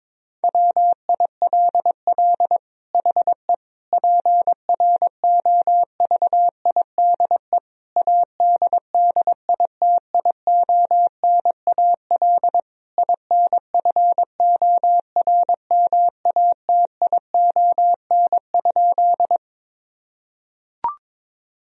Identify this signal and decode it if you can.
Morse code, heard as a keyed tone, about 22 wpm, 700 Hz. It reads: WILL HE PROVIDE ADDITIONAL INFORMATION?
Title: Sentences from Top 1000 Words - Review Part 4 - 22wpm